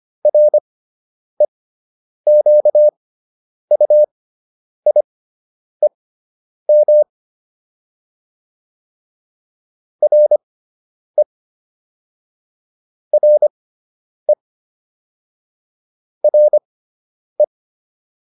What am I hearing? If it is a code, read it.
Morse code, heard as a keyed tone, about 25 wpm, spaced as for 9 wpm, 600 Hz. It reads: REQUIEM RE RE RE